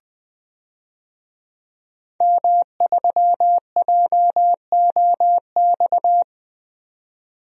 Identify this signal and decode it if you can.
Morse code, heard as a keyed tone, about 20 wpm, 700 Hz. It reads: M3JOX
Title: DXCC-100 Call Signs - 5 Characters - 20wpm